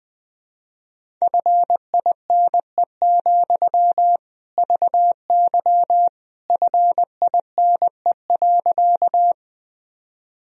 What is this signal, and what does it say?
Morse code, heard as a keyed tone, about 20 wpm, 700 Hz. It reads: FINE, VY FINE.
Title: Sentences from Top 500 Words and Abbreviations 20wpm